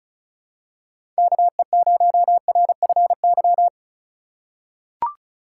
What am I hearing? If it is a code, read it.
Morse code, heard as a keyed tone, about 34 wpm, 700 Hz. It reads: KE0RFY
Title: USA Call Signs - 4x Character Spacing - 35wpm